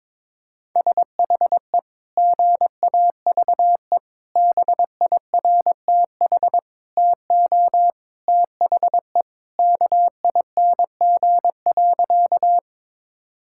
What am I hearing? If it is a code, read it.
Morse code, heard as a keyed tone, about 22 wpm, 700 Hz. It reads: SHE GAVE BIRTH TO THE KING.
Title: Sentences from Top 1000 Words - Review Part 5 - 22wpm